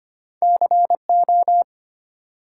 Morse code, heard as a keyed tone, about 25 wpm, 700 Hz. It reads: CO
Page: morse-code-ninja-practice-25wpm